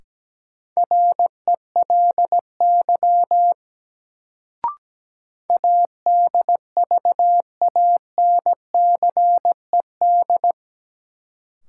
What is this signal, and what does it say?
Morse code, heard as a keyed tone, about 17 wpm, 700 Hz. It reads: RELY ADVANCED
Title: Sentences from Top 1200 Words - New Word Drill - 17wpm